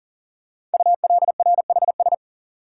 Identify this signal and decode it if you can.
Morse code, heard as a keyed tone, about 42 wpm, 700 Hz. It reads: ULRHS
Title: Code Groups - 5 Letters - 40wpm